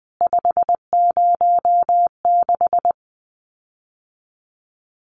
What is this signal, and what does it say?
Morse code, heard as a keyed tone, about 20 wpm, 700 Hz. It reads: 506
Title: CWT Exchange 20wpm